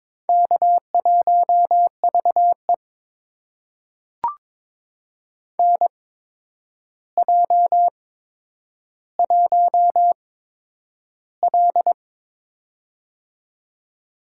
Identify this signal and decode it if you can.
Morse code, heard as a keyed tone, about 22 wpm, spaced as for 9 wpm, 700 Hz. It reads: K1VE NJ1L
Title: USA Call Signs - 4 Characters - 8x Character Spacing - 22wpm